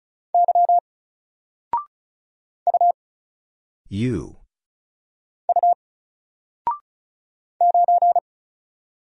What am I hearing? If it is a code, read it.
Morse code, heard as a keyed tone, about 36 wpm, 700 Hz. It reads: Y U U 9
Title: Single Letter-Number Alt 35wpm